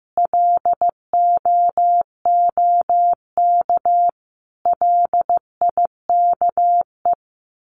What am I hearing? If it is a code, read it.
Morse code, heard as a keyed tone, about 15 wpm, 700 Hz. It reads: LOOK LIKE
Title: Sets of 5 Words Encore2 15wpm